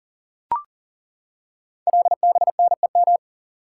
Morse code, heard as a keyed tone, about 38 wpm, 700 Hz. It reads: LBNEM
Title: Code Groups - 5 Letters - 40wpm